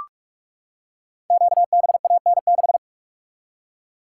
Morse code, heard as a keyed tone, about 44 wpm, 700 Hz. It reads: QBAN6
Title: Code Groups - 5 Letter:Numbers - 45wpm